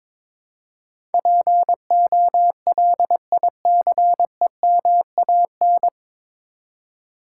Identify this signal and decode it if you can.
Morse code, heard as a keyed tone, about 22 wpm, 700 Hz. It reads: POLICEMAN
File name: 9 Letter Words - First 3 Letters Given - 22wpm